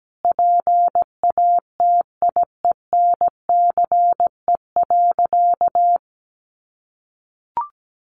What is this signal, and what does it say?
Morse code, heard as a keyed tone, about 17 wpm, 700 Hz. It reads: PATIENCE.
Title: Sentences from Top 1300 Words 17wpm